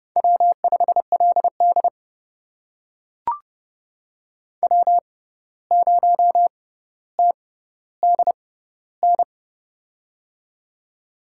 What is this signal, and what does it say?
Morse code, heard as a keyed tone, about 30 wpm, spaced as for 14 wpm, 700 Hz. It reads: W5LD W0TDN
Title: USA Call Signs - 6x Character Spacing - 30wpm